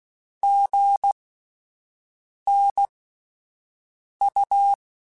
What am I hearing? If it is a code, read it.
Morse code, heard as a keyed tone, about 16 wpm, spaced as for 5 wpm, 800 Hz. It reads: GNU
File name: Spaced Alphas / Spaced Alphas mp3s / Gs sp @10